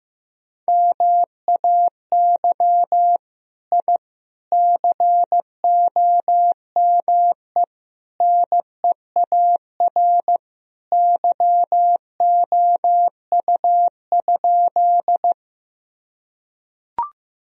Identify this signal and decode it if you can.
Morse code, heard as a keyed tone, about 14 wpm, 700 Hz. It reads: MAY I COME NEAR YOU?
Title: Sentences from Top 1000 Words - Review Part 1 - 15wpm